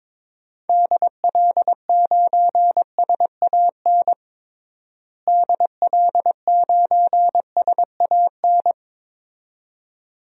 Morse code, heard as a keyed tone, about 22 wpm, 700 Hz. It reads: DL9SAN DL9SAN
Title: DXCC-100 Call Signs - 6 Characters - Repeated 2x - 22wpm